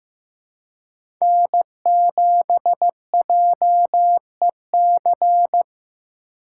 Morse code, heard as a keyed tone, about 15 wpm, 700 Hz. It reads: N7JEC
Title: USA Call Signs - 5 Characters - 3x Character Spacing - 15wpm